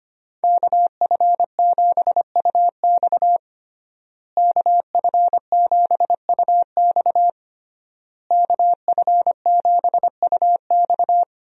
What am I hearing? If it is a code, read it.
Morse code, heard as a keyed tone, about 25 wpm, 700 Hz. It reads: KF7UX KF7UX KF7UX
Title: USA Call Signs - 5 Characters - Repeated 3x - 25wpm